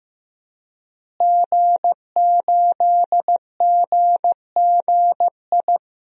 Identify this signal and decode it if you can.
Morse code, heard as a keyed tone, about 15 wpm, 700 Hz. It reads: G8GGI
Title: DXCC-100 Call Signs - 5 Characters - Repeated 2x - 15wpm